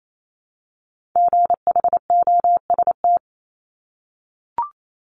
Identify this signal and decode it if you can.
Morse code, heard as a keyed tone, about 28 wpm, 700 Hz. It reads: GHOST
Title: Multi-letter Phonemes Repeated 3x 28wpm